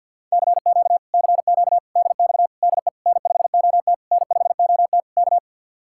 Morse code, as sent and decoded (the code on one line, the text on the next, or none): -.-. --.- / -.-. --.- / -.. -..- / -.. / -. ..... --- - / -. ..... --- - / -.-
CQ CQ DX DE N5OT N5OT K